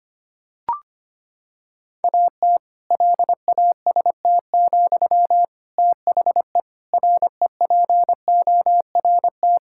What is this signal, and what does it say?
Morse code, heard as a keyed tone, about 25 wpm, 700 Hz. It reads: AT LAST, THE REPORT
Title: Sentences from Top 1000 Words - Review Part 6 - 25wpm